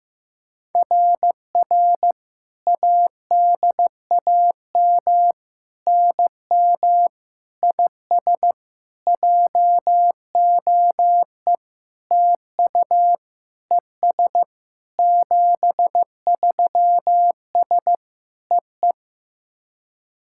Morse code, heard as a keyed tone, about 16 wpm, 700 Hz. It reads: RR ADAM NM IS JOE TU ES 73S EE
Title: SOTA QSO Practice 15wpm